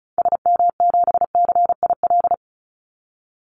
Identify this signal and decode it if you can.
Morse code, heard as a keyed tone, about 34 wpm, 700 Hz. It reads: SM7CIL
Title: Call Signs Repeated 2x - 35wpm